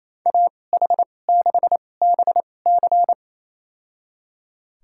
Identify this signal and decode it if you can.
Morse code, heard as a keyed tone, about 28 wpm, spaced as for 20 wpm, 700 Hz. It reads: AH6BC
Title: USA Call Signs - 2x Character Spacing - 28wpm